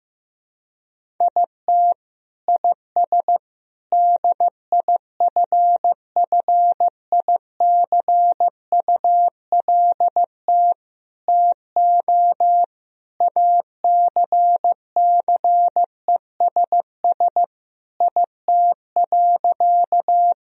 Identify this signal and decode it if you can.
Morse code, heard as a keyed tone, about 15 wpm, 700 Hz. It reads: IT IS DIFFICULT TO ACCESS IT.
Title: Sentences from Top 1000 Words - Review Part 4 - 15wpm